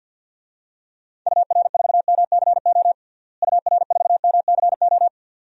Morse code, heard as a keyed tone, about 53 wpm, 700 Hz. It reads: UR4MCO UR4MCO
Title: DXCC-100 Call Signs - 6 Characters - Repeated 2x - 50wpm